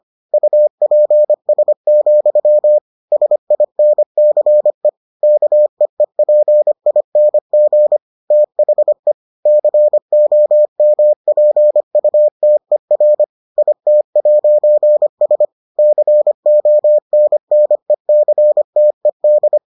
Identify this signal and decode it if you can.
Morse code, heard as a keyed tone, about 25 wpm, 600 Hz. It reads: UPS, SINCE KEEPING THE COMPUTER IT'S CONNECTED